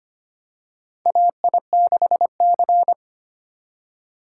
Morse code, heard as a keyed tone, about 25 wpm, 700 Hz. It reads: AI6C
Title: USA Call Signs - 6x Character Spacing - 25wpm